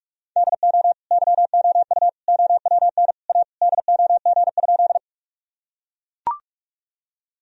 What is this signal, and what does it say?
Morse code, heard as a keyed tone, about 41 wpm, 700 Hz. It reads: DO YOU OWN A DOG?
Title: Sentences from Top 1000 Words - Review Part 2 - 45wpm